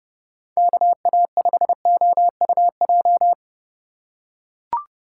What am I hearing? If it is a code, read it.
Morse code, heard as a keyed tone, about 30 wpm, 700 Hz. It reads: KA5OUJ
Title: USA Call Signs - 6 Characters - 4x Character Spacing - 30wpm